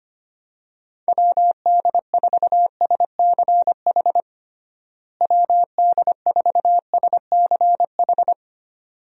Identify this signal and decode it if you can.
Morse code, heard as a keyed tone, about 25 wpm, 700 Hz. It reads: WD4SCH WD4SCH
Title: USA Call Signs - 6 Characters - Repeated 2x - 25wpm